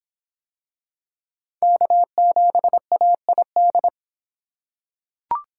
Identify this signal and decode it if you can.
Morse code, heard as a keyed tone, about 24 wpm, 700 Hz. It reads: K7AID